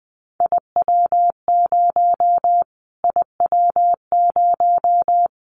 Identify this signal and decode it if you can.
Morse code, heard as a keyed tone, about 20 wpm, 700 Hz. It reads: IW0 IW0
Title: Callsign Prefixes for Top 25 DXCC - Repeated 3x - 20wpm